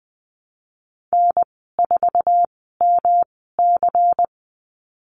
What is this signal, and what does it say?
Morse code, heard as a keyed tone, about 19 wpm, spaced as for 14 wpm, 700 Hz. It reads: N4MC
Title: 128 Lesson 21g - Number 6 - Call Signs 20wpm